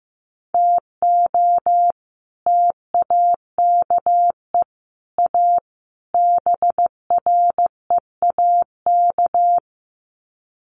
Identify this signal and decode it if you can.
Morse code, heard as a keyed tone, about 15 wpm, 700 Hz. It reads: TO TAKE A BREAK
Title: Sets of 4 Words 15wpm